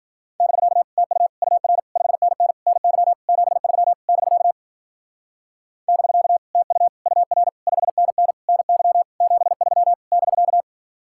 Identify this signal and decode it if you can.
Morse code, heard as a keyed tone, about 50 wpm, 700 Hz. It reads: <BK> TU UR 5NN NY 73 <BK> <BK> TU UR 5NN NY 73 <BK>
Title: Speed-Racing Common QSO Phrases - 40wpm